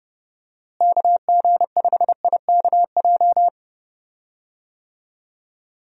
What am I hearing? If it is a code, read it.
Morse code, heard as a keyed tone, about 30 wpm, 700 Hz. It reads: KG5IKJ
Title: USA Call Signs - 6 Characters - Repeated 2x - 30wpm